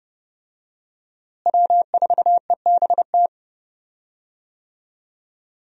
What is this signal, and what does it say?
Morse code, heard as a keyed tone, about 30 wpm, 700 Hz. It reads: W4EBT
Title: USA Call Signs - 5 Characters - Repeated 3x - 30wpm